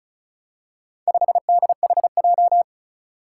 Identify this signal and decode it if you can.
Morse code, heard as a keyed tone, about 36 wpm, 700 Hz. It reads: 5DHJ
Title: Four Challenging Letter-Number Combinations 35wpm